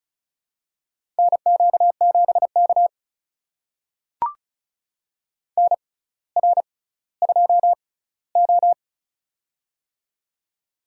Morse code, heard as a keyed tone, about 36 wpm, spaced as for 18 wpm, 700 Hz. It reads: NQ7K NR2O
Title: USA Call Signs - 4 Characters - 6x Character Spacing - 35wpm